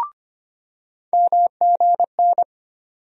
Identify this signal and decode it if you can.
Morse code, heard as a keyed tone, about 25 wpm, 700 Hz. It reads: MGN
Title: USA Call Sign Postfixes - 3 Letters 25wpm